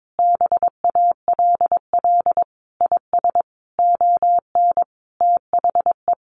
Morse code, heard as a keyed tone, about 22 wpm, 700 Hz. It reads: BALL IS ON THE
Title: Easy Phrases 22wpm